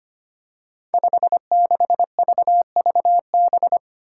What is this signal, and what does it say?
Morse code, heard as a keyed tone, about 25 wpm, 700 Hz. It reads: 56VVB
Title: Five Challenging Letter-Number Combinations 25wpm